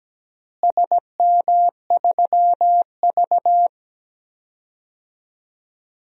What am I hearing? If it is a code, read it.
Morse code, heard as a keyed tone, about 17 wpm, 700 Hz. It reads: SM3V
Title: DXCC-100 Call Signs - 4 Characters - Repeated 2x - 17wpm